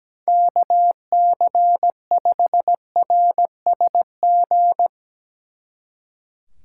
Morse code, heard as a keyed tone, about 18 wpm, 700 Hz. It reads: KC5RSG
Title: Speed Racing - USA 2x3 Call Signs 17wpm